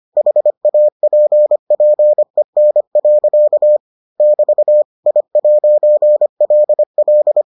Morse code, heard as a keyed tone, about 25 wpm, 600 Hz. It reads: HAPPEN. <BT> I'LL